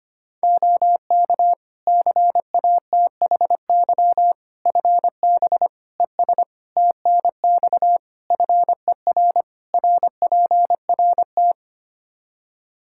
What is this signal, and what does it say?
Morse code, heard as a keyed tone, about 25 wpm, 700 Hz. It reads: OK CATHY FB ES TNX FER RPRT
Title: K1USN SST QSO Practice 25wpm